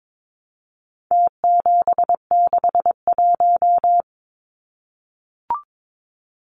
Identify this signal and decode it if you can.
Morse code, heard as a keyed tone, about 22 wpm, 700 Hz. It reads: T761
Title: Numbers - Contest Style - 4-Digit 22wpm